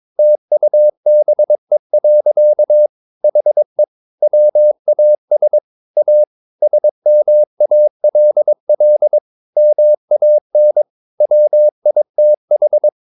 Morse code, heard as a keyed tone, about 22 wpm, 600 Hz. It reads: TUBE. HE WAS A SMALL MAN WITH
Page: Romanen Never af Ken Follet på engelsk.